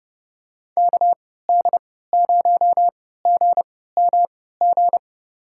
Morse code, heard as a keyed tone, about 30 wpm, spaced as for 17 wpm, 700 Hz. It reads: KD0GMG
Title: USA Call Signs - 6 Characters - 3x Character Spacing - 30wpm